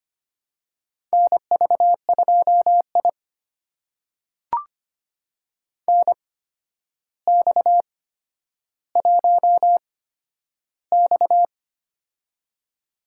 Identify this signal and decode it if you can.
Morse code, heard as a keyed tone, about 25 wpm, spaced as for 11 wpm, 700 Hz. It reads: NV2I NX1X
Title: USA Call Signs - 4 Characters - 8x Character Spacing - 25wpm